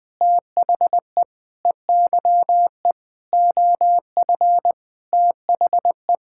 Morse code, heard as a keyed tone, about 20 wpm, 700 Hz. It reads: THE EYE OF THE
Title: Sets of 8 Words - 20wpm